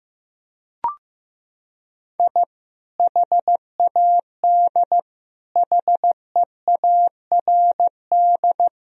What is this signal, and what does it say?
Morse code, heard as a keyed tone, about 15 wpm, 700 Hz. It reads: I HAD HEARD
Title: Sentences from Top 1000 Words - Review Part 1 - 15wpm